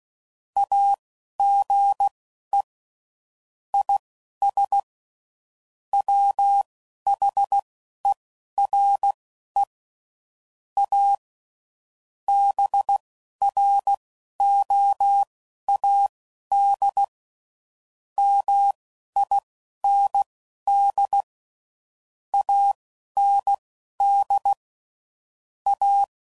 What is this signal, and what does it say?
Morse code, heard as a keyed tone, about 16 wpm, spaced as for 11 wpm, 800 Hz. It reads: AGE IS WHERE A BROAD MIND AND A